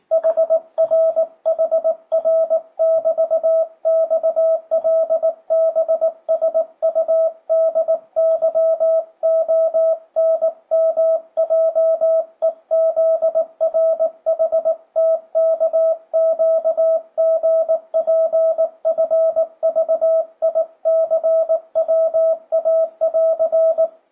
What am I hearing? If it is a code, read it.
Morse code, heard as a keyed tone, about 18 wpm, 660 Hz. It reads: HRHR<BT>XLBSUDYONMJEZRHTKQGPFVICWA<AR>
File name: 2025cw / speedup / 19wpm